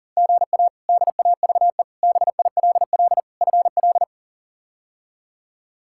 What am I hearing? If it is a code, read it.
Morse code, heard as a keyed tone, about 40 wpm, 700 Hz. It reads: GA DAVE BILL FL
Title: K1USN SST QSO Practice - 40wpm